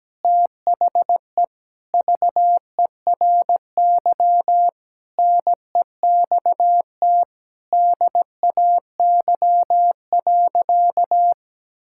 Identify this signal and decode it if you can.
Morse code, heard as a keyed tone, about 17 wpm, 700 Hz. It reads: THE VERY NEXT DAY.
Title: Sentences from Top 1000 Words - Review Part 6 - 17wpm